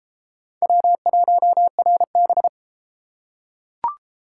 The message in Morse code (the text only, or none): W1RB